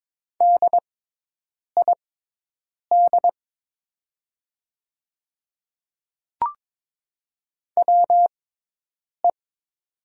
Morse code, heard as a keyed tone, about 22 wpm, spaced as for 8 wpm, 700 Hz. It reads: DID WE
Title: 6x Character Spacing - Top 100 Words - 22wpm